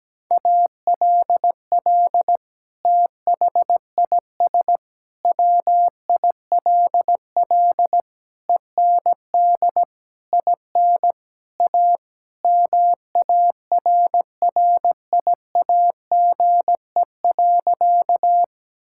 Morse code, heard as a keyed tone, about 17 wpm, 700 Hz. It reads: ALL THIS WILL END IN A MARRIAGE.
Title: Sentences from Top 1000 Words - Review Part 5 - 17wpm